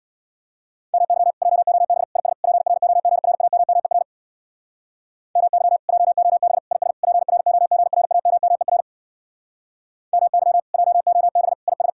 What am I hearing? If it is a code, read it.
Morse code, heard as a keyed tone, about 62 wpm, 700 Hz. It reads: MY JOB IS PROGRAMMER MY JOB IS PROGRAMMER MY JOB IS
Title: Speed-Racing Common QSO Phrases - 50wpm